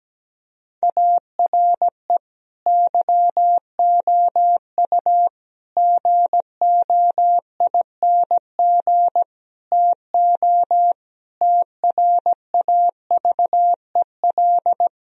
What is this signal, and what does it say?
Morse code, heard as a keyed tone, about 17 wpm, 700 Hz. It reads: ARE YOU GOING TO TRAVEL
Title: Sentences from Top 1000 Words - Review Part 3 - 17wpm